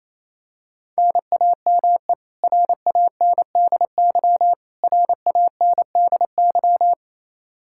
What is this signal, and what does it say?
Morse code, heard as a keyed tone, about 28 wpm, 700 Hz. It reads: NAME RANDY RANDY
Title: Practice QSO - Exchange 1 - Beginning - 28wpm